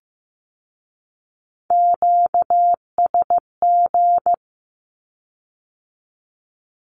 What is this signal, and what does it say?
Morse code, heard as a keyed tone, about 15 wpm, 700 Hz. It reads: QSG
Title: Q-Codes with Spoken Meaning - 15wpm